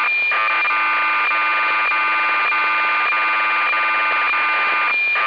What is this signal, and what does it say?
Cassette data (Kansas City type).